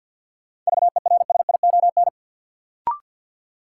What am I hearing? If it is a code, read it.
Morse code, heard as a keyed tone, about 48 wpm, 700 Hz. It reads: VERSION
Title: Sentences from Top 1400 Words - New Word Drill - 50wpm